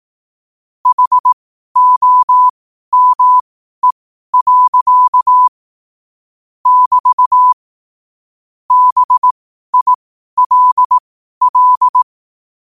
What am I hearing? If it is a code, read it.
Morse code, heard as a keyed tone, about 18 wpm, spaced as for 12 wpm, 1000 Hz. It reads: HOME. <BT> BILL
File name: 2026-01-07-12wpm